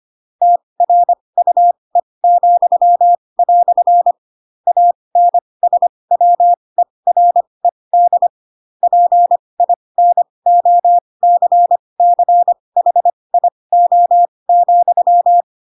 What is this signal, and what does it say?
Morse code, heard as a keyed tone, about 25 wpm, spaced as for 20 wpm, 700 Hz. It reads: TRUE," ANSWERED PINOCCHIO,
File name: Pinocchio / Pinocchio ch8 15 25 ik1hge